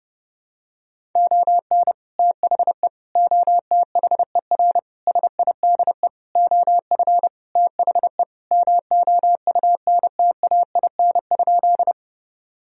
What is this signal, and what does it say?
Morse code, heard as a keyed tone, about 32 wpm, 700 Hz. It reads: ON THE OTHER SIDE OF THE MOUNTAIN?
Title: Sentences from Top 1000 Words - Review Part 1 - 30wpm